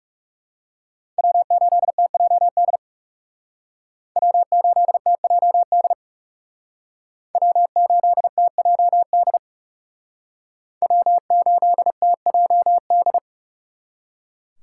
Morse code, heard as a keyed tone, about 36 wpm, 700 Hz. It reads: W8TJD W8TJD W8TJD W8TJD
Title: Speed Racing - USA 1x3 Call Signs 30wpm